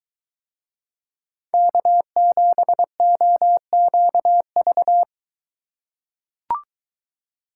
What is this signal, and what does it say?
Morse code, heard as a keyed tone, about 22 wpm, 700 Hz. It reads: K7OQV